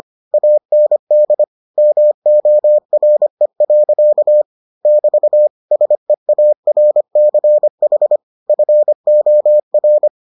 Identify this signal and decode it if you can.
Morse code, heard as a keyed tone, about 25 wpm, 600 Hz. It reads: AND MORE. <BT> SEARCH FOR